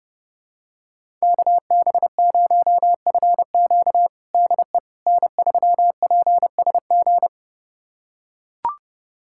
Morse code, heard as a keyed tone, about 32 wpm, 700 Hz. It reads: KB0FQ DE N3PSG
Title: USA Callsign DE Callsign - 5 Characters - 30wpm